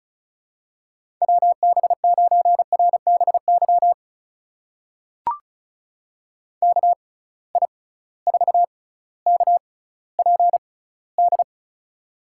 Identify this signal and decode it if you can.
Morse code, heard as a keyed tone, about 36 wpm, spaced as for 18 wpm, 700 Hz. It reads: WB9RBY KI4KPD
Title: USA Call Signs - 6 Characters - 6x Character Spacing - 35wpm